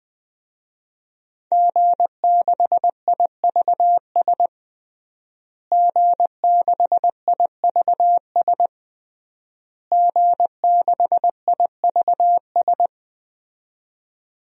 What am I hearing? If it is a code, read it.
Morse code, heard as a keyed tone, about 20 wpm, 700 Hz. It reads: G6IVS G6IVS G6IVS
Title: DXCC-100 Call Signs - 5 Characters - Repeated 3x - 20wpm